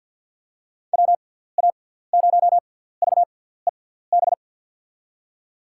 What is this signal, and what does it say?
Morse code, heard as a keyed tone, about 49 wpm, spaced as for 17 wpm, 700 Hz. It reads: WA0VEB
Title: USA Call Signs - 6 Characters - 6x Character Spacing - 50wpm